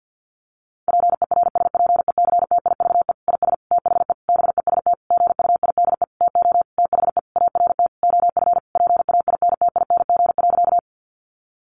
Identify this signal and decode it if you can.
Morse code, heard as a keyed tone, about 50 wpm, 700 Hz. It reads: PERSPECTIVE IS THE BEST GUIDE TO THE ART OF PAINTING.
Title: Sentences from Top 1200 Words 50wpm